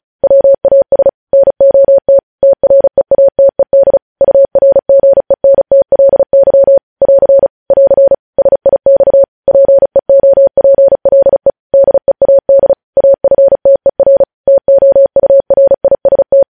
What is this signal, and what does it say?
Morse code, heard as a keyed tone, about 36 wpm, 550 Hz. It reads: WAS NOT TREATED URGENTLY <AR> <AR> SIX PEOPLE DEAD AFTER TOURIST
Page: News headlines in Morse code at 35 wpm